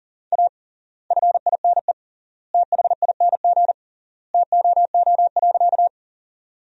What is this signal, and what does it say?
Morse code, heard as a keyed tone, about 40 wpm, 700 Hz. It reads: A FINE THING TOO.
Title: 3x Word-Spacing - Sentences from Top 500 Words - 40wpm